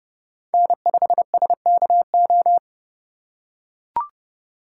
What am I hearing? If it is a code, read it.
Morse code, heard as a keyed tone, about 29 wpm, 700 Hz. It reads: N5SKO